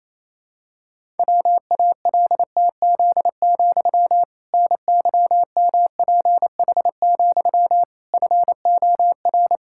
Morse code, heard as a keyed tone, about 28 wpm, 700 Hz. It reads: WALTZ, NYMPH, FOR
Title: Pangram Challenge v2 - 28wpm